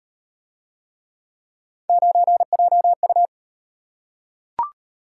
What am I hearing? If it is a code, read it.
Morse code, heard as a keyed tone, about 32 wpm, 700 Hz. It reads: 9JU